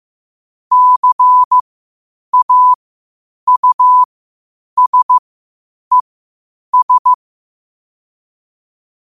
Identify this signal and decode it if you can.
Morse code, heard as a keyed tone, about 15 wpm, spaced as for 8 wpm, 1000 Hz. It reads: CAUSES
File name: qotdfiles / 8wpm